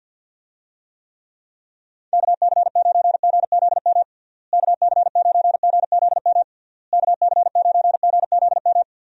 Morse code, heard as a keyed tone, about 52 wpm, 700 Hz. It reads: KC9GZM KC9GZM KC9GZM
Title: USA Call Signs - 6 Characters - Repeated 3x - 50wpm